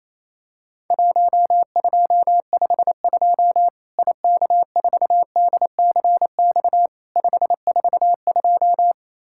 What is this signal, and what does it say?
Morse code, heard as a keyed tone, about 27 wpm, 700 Hz. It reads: 1252 IK4DCX 542
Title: Worked All Europe DX Contest - QTC Exchange - 28wpm